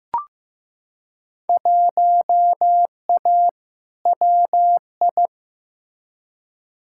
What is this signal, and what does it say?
Morse code, heard as a keyed tone, about 16 wpm, 700 Hz. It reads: 1A WI
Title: Field Day Exchanges - 15wpm